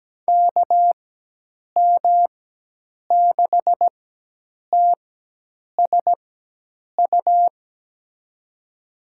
Morse code, heard as a keyed tone, about 17 wpm, spaced as for 8 wpm, 700 Hz. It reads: KM6TSU
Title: USA Call Signs - 4x Character Spacing - 17wpm